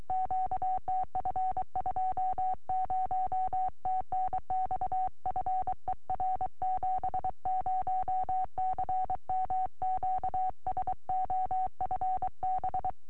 Morse code, heard as a keyed tone, about 23 wpm, 750 Hz. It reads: QTF 20 TNX FER 70CM QSO FB
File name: au / dj4tc 2m